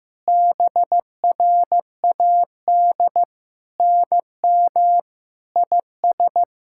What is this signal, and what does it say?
Morse code, heard as a keyed tone, about 15 wpm, 700 Hz. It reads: BRAD NM IS
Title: SOTA QSO Practice 15wpm